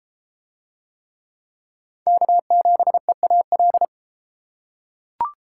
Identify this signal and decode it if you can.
Morse code, heard as a keyed tone, about 30 wpm, 700 Hz. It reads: K7EAL